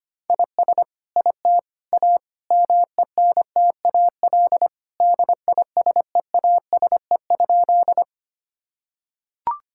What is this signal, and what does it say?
Morse code, heard as a keyed tone, about 25 wpm, 700 Hz. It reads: IS IT A MENTAL DISEASE?
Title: Sentences from Top 1000 Words - Review Part 6 - 25wpm